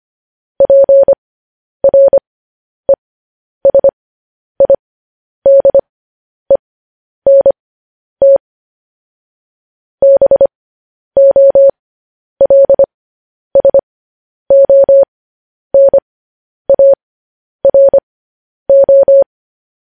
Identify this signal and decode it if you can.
Morse code, heard as a keyed tone, about 25 wpm, spaced as for 10 wpm, 550 Hz. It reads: PRESIDENT BOLSONARO
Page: news headlines in morse code at 10 wpm